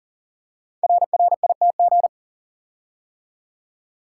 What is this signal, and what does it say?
Morse code, heard as a keyed tone, about 40 wpm, 700 Hz. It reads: RRITG
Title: Code Groups - 5 Letter:Numbers - 40wpm